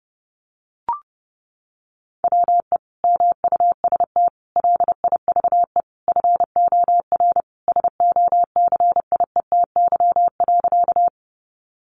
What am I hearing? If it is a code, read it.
Morse code, heard as a keyed tone, about 28 wpm, 700 Hz. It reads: WE MUST LIVE FOR SOCIETY.
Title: Sentences from Top 700 Words 30wpm